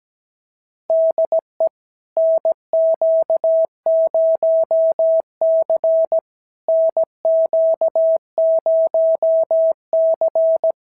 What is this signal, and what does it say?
Morse code, heard as a keyed tone, about 17 wpm, 650 Hz. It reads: DE NQ0C NQ0C
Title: SOTA QSO Practice 17wpm